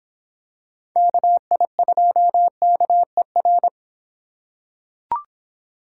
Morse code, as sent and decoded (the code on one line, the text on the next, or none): -.- .. ..--- -.- . .-.
KI2KER